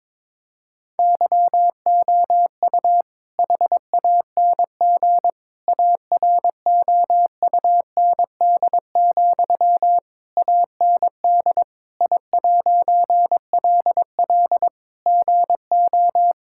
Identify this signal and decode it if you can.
Morse code, heard as a keyed tone, about 22 wpm, 700 Hz. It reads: YOU HANG AROUND, AND I'LL GO
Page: morse-code-ninja-practice-22wpm